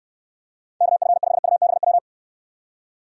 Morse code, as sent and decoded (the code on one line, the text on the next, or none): -.- ..-. ..... ..- -... .--
KF5UBW